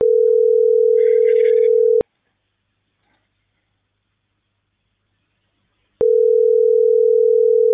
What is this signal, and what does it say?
Ringback tone (North America), 440 + 480 Hz.